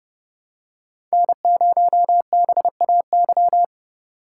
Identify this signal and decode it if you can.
Morse code, heard as a keyed tone, about 30 wpm, 700 Hz. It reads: N0BAY
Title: USA Call Signs - 5 Characters - 4x Character Spacing - 30wpm